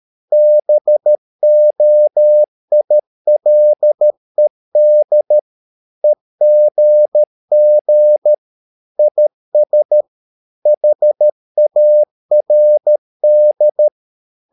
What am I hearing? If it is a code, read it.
Morse code, heard as a keyed tone, about 13 wpm, 600 Hz. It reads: BOILED EGG IS HARD
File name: mp3 files / Lexiphiles - 13wpm